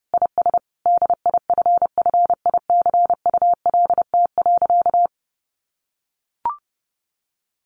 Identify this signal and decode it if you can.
Morse code, heard as a keyed tone, about 30 wpm, 700 Hz. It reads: IS DIFFICULT.
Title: Sentences from Top 1300 Words 30wpm